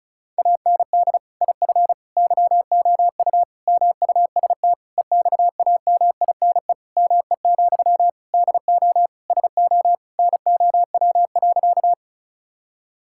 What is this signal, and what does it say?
Morse code, heard as a keyed tone, about 36 wpm, 700 Hz. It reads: AND IF YOU MUST EXAMINE ME, DO SO NOW.
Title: Sentences from Top 1600 Words 35wpm